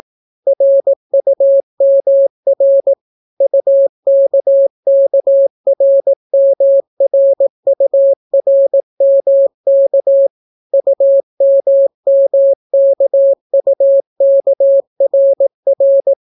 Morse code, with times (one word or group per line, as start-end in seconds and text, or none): 0.5-2.9 RUMR
3.4-10.3 UKKRMRURMK
10.7-16.1 UMMKUKRR